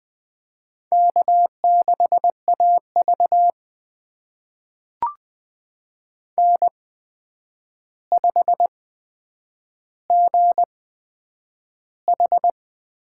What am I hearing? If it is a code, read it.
Morse code, heard as a keyed tone, about 20 wpm, spaced as for 9 wpm, 700 Hz. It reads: K6AV N5GH
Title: USA Call Signs - 4 Characters - 8x Character Spacing - 20wpm